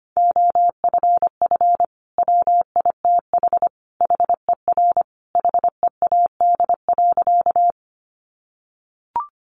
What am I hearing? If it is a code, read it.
Morse code, heard as a keyed tone, about 25 wpm, 700 Hz. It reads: OFF WITH HER HEAD.
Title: Sentences from Top 300 Words 25wpm